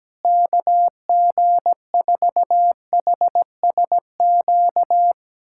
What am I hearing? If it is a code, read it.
Morse code, heard as a keyed tone, about 17 wpm, 700 Hz. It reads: KG4HSQ
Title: USA Callsign DE Callsign 17wpm